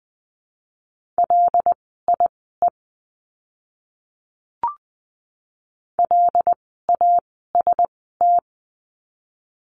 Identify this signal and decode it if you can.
Morse code, heard as a keyed tone, about 20 wpm, spaced as for 15 wpm, 700 Hz. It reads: LIE LAST
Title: 075 Lesson 13f - Letter L - Words 20wpm